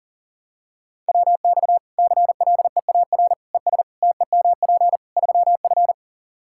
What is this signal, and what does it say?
Morse code, heard as a keyed tone, about 37 wpm, 700 Hz. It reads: WX CLEAR ES TEMP 3F
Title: Practice QSO - Exchange 2 - Info - 40wpm